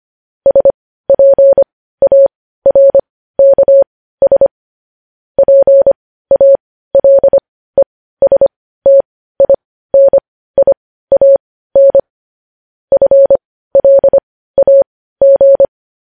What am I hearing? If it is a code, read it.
Morse code, heard as a keyed tone, about 25 wpm, spaced as for 15 wpm, 550 Hz. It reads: SPARKS PALESTINIAN FLAG
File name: podcast / News headlines-1x-25x15